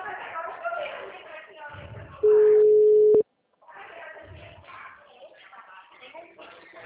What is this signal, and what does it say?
Ringback tone (Europe, 425 Hz), with speech.